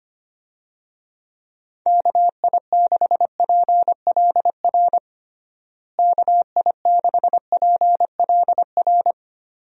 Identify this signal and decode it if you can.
Morse code, heard as a keyed tone, about 25 wpm, 700 Hz. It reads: KI6PLR KI6PLR
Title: USA Call Signs - 6 Characters - Repeated 2x - 25wpm